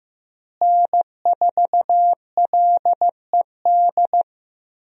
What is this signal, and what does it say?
Morse code, heard as a keyed tone, about 15 wpm, 700 Hz. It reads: N4LED
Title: USA Call Signs - 5 Characters - Repeated 2x - 15wpm